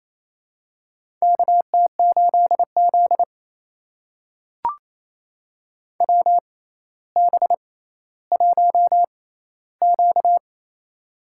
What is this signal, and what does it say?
Morse code, heard as a keyed tone, about 28 wpm, spaced as for 14 wpm, 700 Hz. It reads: KT8Z WB1Q
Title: USA Call Signs - 4 Characters - 6x Character Spacing - 28wpm